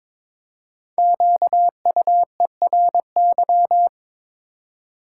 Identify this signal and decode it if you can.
Morse code, heard as a keyed tone, about 22 wpm, 700 Hz. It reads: QUERY
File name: 5 Letter Words - First 3 Letters Given - 22wpm